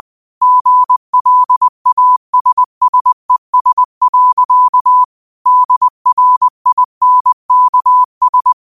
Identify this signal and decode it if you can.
Morse code, heard as a keyed tone, about 20 wpm, 1000 Hz. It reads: GLASSES. DRINKS
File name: qotdfiles / 20wpm